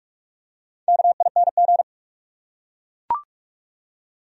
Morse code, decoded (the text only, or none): KING